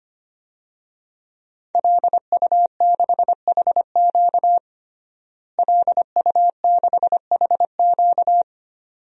Morse code, heard as a keyed tone, about 25 wpm, 700 Hz. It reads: LU6HQ LU6HQ
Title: DXCC-100 Call Signs - 5 Characters - Repeated 2x - 25wpm